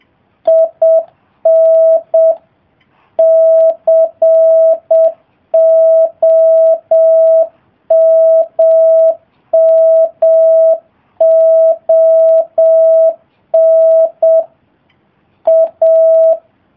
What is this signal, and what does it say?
Morse code, heard as a keyed tone, about 7 wpm, 650 Hz. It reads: IN COMMON A